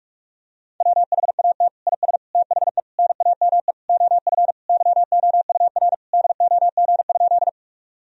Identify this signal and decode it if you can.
Morse code, heard as a keyed tone, about 41 wpm, 700 Hz. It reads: WHAT IS THE NAME OF YOUR DOG?
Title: Sentences from Top 1000 Words - Review Part 2 - 45wpm